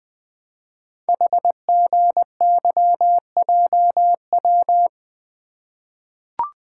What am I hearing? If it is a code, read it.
Morse code, heard as a keyed tone, about 20 wpm, 700 Hz. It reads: HGYJW
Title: Code Groups - 5 Letters - 20wpm